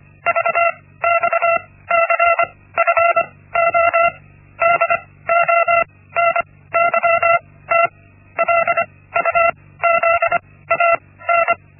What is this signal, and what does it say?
Morse code, heard as a keyed tone, about 25 wpm, spaced as for 17 wpm, 680 Hz. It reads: VXCFO DONYT LUZAN
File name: Linksmakalnis-1